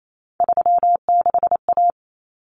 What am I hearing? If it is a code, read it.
Morse code, heard as a keyed tone, about 28 wpm, 700 Hz. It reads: 36A
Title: Common Cut Numbers - 3-Digit 28wpm